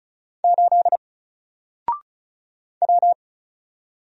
Morse code, heard as a keyed tone, about 36 wpm, 700 Hz. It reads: 8 W
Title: Single Letter-Number 35wpm